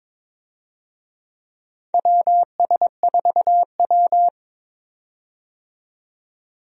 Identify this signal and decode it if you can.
Morse code, heard as a keyed tone, about 21 wpm, 700 Hz. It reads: WS4W
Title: USA Call Signs - 4 Characters - 22wpm